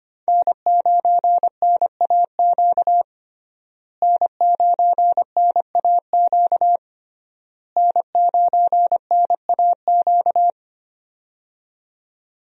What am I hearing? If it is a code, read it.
Morse code, heard as a keyed tone, about 25 wpm, 700 Hz. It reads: N9NAQ N9NAQ N9NAQ
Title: USA Call Signs - 5 Characters - Repeated 3x - 25wpm